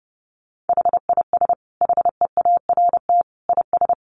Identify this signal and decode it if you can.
Morse code, heard as a keyed tone, about 30 wpm, 700 Hz. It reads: HIS HEART IS
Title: Sentences from Top 1200 Words 30wpm